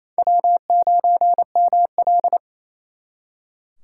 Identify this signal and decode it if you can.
Morse code, heard as a keyed tone, about 34 wpm, 700 Hz. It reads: W9ML
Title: Speed Racing - USA 1x2 Call Signs 28wpm